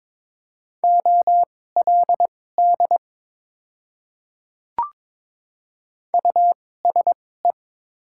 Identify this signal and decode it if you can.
Morse code, heard as a keyed tone, about 21 wpm, spaced as for 16 wpm, 700 Hz. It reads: OLD USE
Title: 2x Character Spacing - Top 200 Words - 22wpm